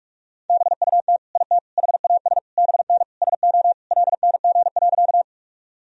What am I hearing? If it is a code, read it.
Morse code, heard as a keyed tone, about 46 wpm, 700 Hz. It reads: BUT IT HAS BN SO LNG.
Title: Sentences from Top 100 Words and Abbreviations 45wpm